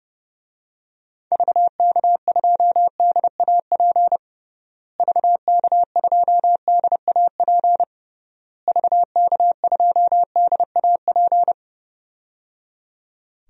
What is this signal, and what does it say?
Morse code, heard as a keyed tone, about 30 wpm, 700 Hz. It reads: VK2DAP VK2DAP VK2DAP
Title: DXCC-100 Call Signs - 6 Characters - Repeated 3x - 30wpm